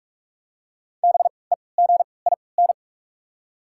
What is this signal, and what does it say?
Morse code, heard as a keyed tone, about 45 wpm, spaced as for 25 wpm, 690 Hz. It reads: BEGIN
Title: 3x Character Spacing - Top 500 Words - 45wpm